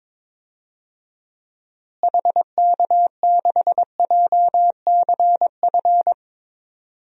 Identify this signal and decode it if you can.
Morse code, heard as a keyed tone, about 22 wpm, 700 Hz. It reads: HK6JCF
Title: DXCC-100 Call Signs - 6 Characters - Repeated 2x - 22wpm